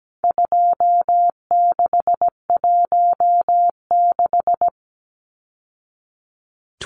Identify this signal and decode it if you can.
Morse code, heard as a keyed tone, about 17 wpm, 700 Hz. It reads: 2616
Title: Numbers - 4-Digit 17wpm